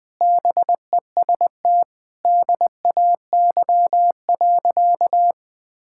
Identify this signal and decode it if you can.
Morse code, heard as a keyed tone, about 21 wpm, 700 Hz. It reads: BEST DAY.
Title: Sentences with 4-Word Phrases - 20wpm